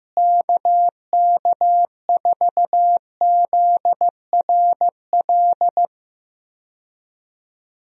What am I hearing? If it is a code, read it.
Morse code, heard as a keyed tone, about 15 wpm, 700 Hz. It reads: KK4ZRL
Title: USA Call Signs - 6 Characters - Repeated 2x - 15wpm